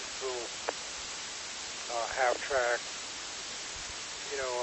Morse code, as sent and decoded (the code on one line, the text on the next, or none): none